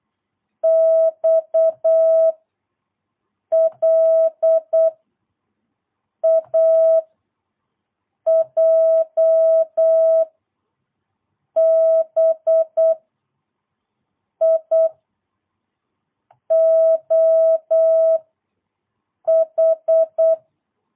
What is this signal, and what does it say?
Morse code, heard as a keyed tone, about 8 wpm, 660 Hz. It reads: X L A J B I O H
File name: jh8cbh / 2022cw